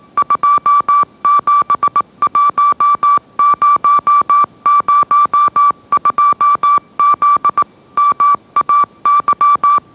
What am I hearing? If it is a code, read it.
Morse code, heard as a keyed tone, about 18 wpm, 1160 Hz. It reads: 271002Z MAY